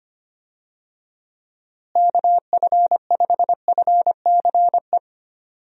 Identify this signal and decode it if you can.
Morse code, heard as a keyed tone, about 25 wpm, 700 Hz. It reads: KF5FCE
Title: USA Call Signs - 6 Characters - Repeated 2x - 25wpm